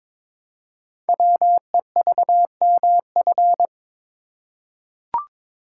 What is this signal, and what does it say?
Morse code, heard as a keyed tone, about 22 wpm, 700 Hz. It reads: WEVMF
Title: Code Groups - 5 Letters - 22wpm